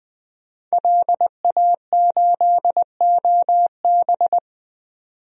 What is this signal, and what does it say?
Morse code, heard as a keyed tone, about 21 wpm, 700 Hz. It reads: LA8OB
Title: DXCC-100 Call Signs - 5 Characters - 20wpm